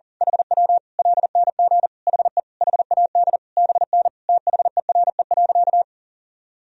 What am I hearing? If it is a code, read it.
Morse code, heard as a keyed tone, about 40 wpm, 700 Hz. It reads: HW LNG HE HAD BN THERE.
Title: Sentences from Top 300 Words and Abbreviations 40wpm